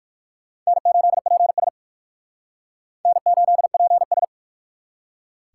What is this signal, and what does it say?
Morse code, heard as a keyed tone, about 55 wpm, 700 Hz. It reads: N8PS N8PS
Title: Speed Racing - USA 1x2 Call Signs 45wpm